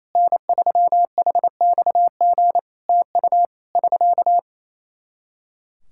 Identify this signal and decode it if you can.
Morse code, heard as a keyed tone, about 27 wpm, 700 Hz. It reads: N3HXG TU <SK>
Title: Practice QSO - Exchange 4 - Ending - 28wpm